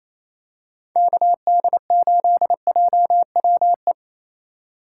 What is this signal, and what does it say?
Morse code, heard as a keyed tone, about 28 wpm, 700 Hz. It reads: KD8JWE
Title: USA Call Signs - 6 Characters - 2x Character Spacing - 28wpm